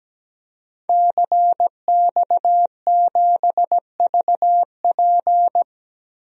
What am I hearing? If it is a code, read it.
Morse code, heard as a keyed tone, about 17 wpm, 700 Hz. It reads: CX7VP
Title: DXCC-100 Call Signs - 5 Characters - Repeated 2x - 17wpm